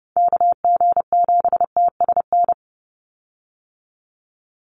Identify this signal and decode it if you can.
Morse code, heard as a keyed tone, about 30 wpm, 700 Hz. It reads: KG7TSN
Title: USA Call Signs 30wpm